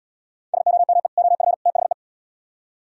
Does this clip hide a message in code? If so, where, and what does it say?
Yes: Morse code from 0.5 to 1.9 s, heard as a keyed tone, about 77 wpm, 700 Hz. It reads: SOME OF THE